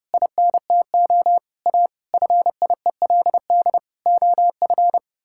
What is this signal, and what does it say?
Morse code, heard as a keyed tone, about 30 wpm, 700 Hz. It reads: INTO A FIELD OF
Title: Pangram Challenge v2 - 30wpm